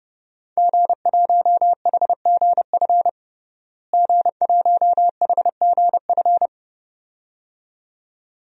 Morse code, heard as a keyed tone, about 30 wpm, 700 Hz. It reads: G1HGF G1HGF
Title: DXCC-100 Call Signs - 5 Characters - Repeated 2x - 30wpm